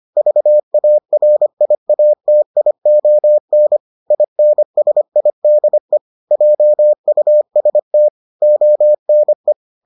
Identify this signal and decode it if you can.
Morse code, heard as a keyed tone, about 25 wpm, 600 Hz. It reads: VARIATION INSIDE JUST ONE